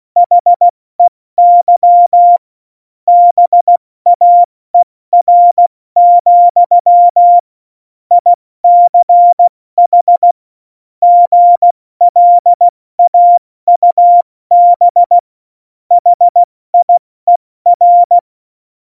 Morse code, heard as a keyed tone, about 16 wpm, 700 Hz. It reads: HEY BAER, ICH GLAUB HIER
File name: Morsekurs / Witze / Witz 17.Juli 2018, 70 F80